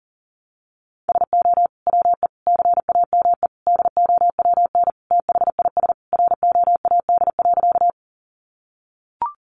Morse code, heard as a keyed tone, about 40 wpm, 700 Hz. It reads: SO WE CAME DOWN THIS ROAD.
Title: Sentences from Top 400 Words 40wpm